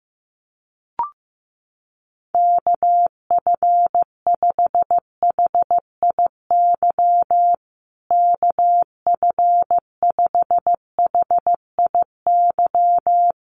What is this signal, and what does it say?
Morse code, heard as a keyed tone, about 15 wpm, 700 Hz. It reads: KF5HIY KF5HIY
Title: USA Call Signs Repeated 3x 15wpm